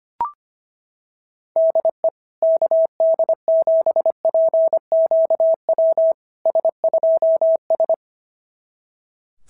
Morse code, heard as a keyed tone, about 25 wpm, 650 Hz. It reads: DE KD7PQW S2S
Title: SOTA QSO Practice 25wpm